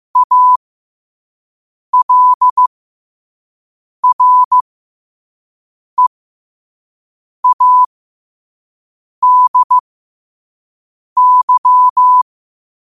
Morse code, heard as a keyed tone, about 15 wpm, spaced as for 5 wpm, 1000 Hz. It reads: ALREADY
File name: qotdfiles / 5wpm